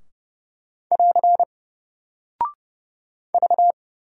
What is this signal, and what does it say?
Morse code, heard as a keyed tone, about 30 wpm, 700 Hz. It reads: <AR> V
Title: Single Character (Letter:Number:Punctuation:Prosign) Alt 30wpm